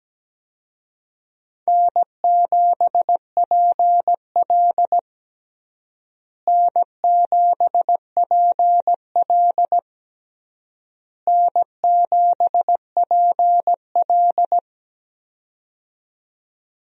Morse code, heard as a keyed tone, about 17 wpm, 700 Hz. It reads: N7PL N7PL N7PL
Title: USA Call Signs - 4 Characters - Repeated 3x - 17wpm